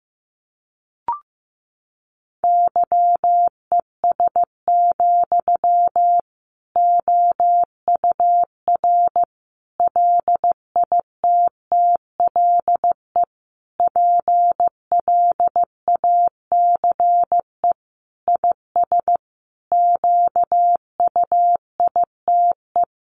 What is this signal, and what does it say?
Morse code, heard as a keyed tone, about 15 wpm, 700 Hz. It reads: YES, OUR LITTLE PLACE IS QUITE
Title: Sentences from Top 700 Words 15wpm